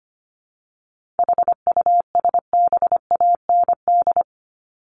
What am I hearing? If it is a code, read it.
Morse code, heard as a keyed tone, about 25 wpm, 700 Hz. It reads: HUSBAND
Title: Top 1501-1750 Words 25wpm